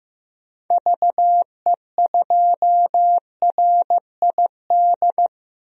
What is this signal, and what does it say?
Morse code, heard as a keyed tone, about 15 wpm, 700 Hz. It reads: VE2RID
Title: DXCC-100 Call Signs - 6 Characters - Repeated 3x - 15wpm